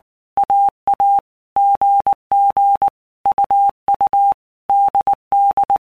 Morse code, heard as a keyed tone, about 20 wpm, 800 Hz. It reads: AA GG UU DD
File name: Basic Letter Groups By Blocks / Groups@18 / AWUNDG@18